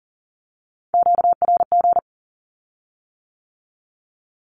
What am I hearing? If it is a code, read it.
Morse code, heard as a keyed tone, about 40 wpm, 700 Hz. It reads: QRG
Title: Q-Codes with Spoken Meaning - 40wpm